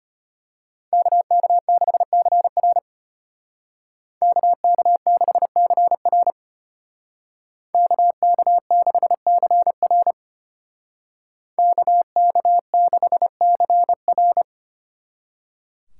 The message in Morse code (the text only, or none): KK6CR KK6CR KK6CR KK6CR